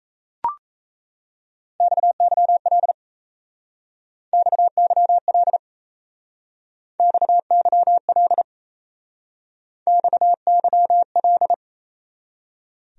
Morse code, heard as a keyed tone, about 35 wpm, 700 Hz. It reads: XYL XYL XYL XYL
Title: Speed-Racing Abbreviations - 28wpm